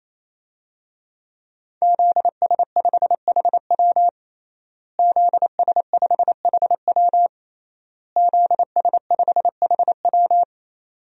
Morse code, heard as a keyed tone, about 28 wpm, 700 Hz. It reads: ZS5HW ZS5HW ZS5HW
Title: DXCC-100 Call Signs - 5 Characters - Repeated 3x - 28wpm